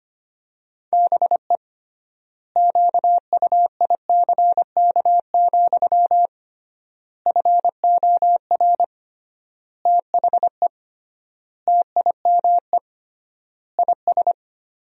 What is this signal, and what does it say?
Morse code, heard as a keyed tone, about 25 wpm, 700 Hz. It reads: BE QUICK, FOR THE TIME IS
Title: 3x Word-Spacing - Sentences from Top 500 Words - 25wpm